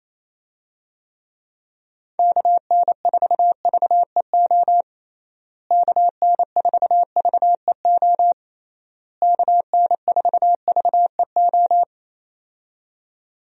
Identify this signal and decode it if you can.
Morse code, heard as a keyed tone, about 28 wpm, 700 Hz. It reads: KN4VEO KN4VEO KN4VEO
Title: USA Call Signs - 6 Characters - Repeated 3x - 28wpm